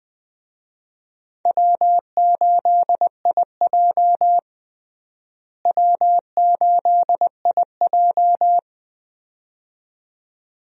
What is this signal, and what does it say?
Morse code, heard as a keyed tone, about 20 wpm, 700 Hz. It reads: W8IJ W8IJ
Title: USA Call Signs - 4 Characters - Repeated 2x - 20wpm